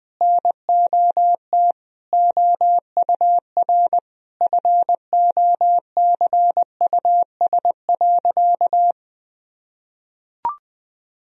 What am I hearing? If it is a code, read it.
Morse code, heard as a keyed tone, about 21 wpm, 700 Hz. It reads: NOT OUR FOCUS.
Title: Sentences from Top 1000 Words - Review Part 4 - 20wpm